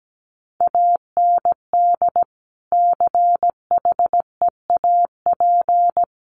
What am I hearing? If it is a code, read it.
Morse code, heard as a keyed tone, about 17 wpm, 700 Hz. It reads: AND CHEAP
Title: Sentences from Top 1200 Words 17wpm